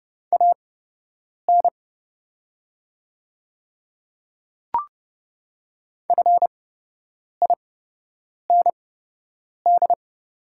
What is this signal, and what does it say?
Morse code, heard as a keyed tone, about 30 wpm, spaced as for 8 wpm, 700 Hz. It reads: AN FIND
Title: 8x Character Spacing - Top 100 Words - 30wpm